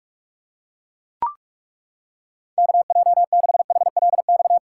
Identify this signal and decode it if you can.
Morse code, heard as a keyed tone, about 42 wpm, 700 Hz. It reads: KJ6HLX
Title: USA Callsign DE Callsign - 6 Characters - 45wpm